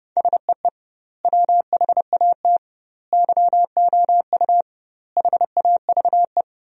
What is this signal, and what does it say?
Morse code, heard as a keyed tone, about 30 wpm, 700 Hz. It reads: SEE WHAT YOU HAVE
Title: 2x Word-Spacing - Sentences from Top 500 Words - 30wpm